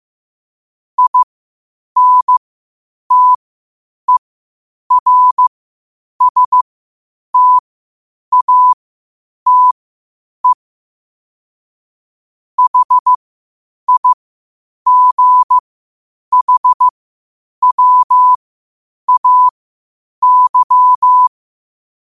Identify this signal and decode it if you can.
Morse code, heard as a keyed tone, about 15 wpm, spaced as for 8 wpm, 1000 Hz. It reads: INTERSTATE HIGHWAY
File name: qotdfiles / 8wpm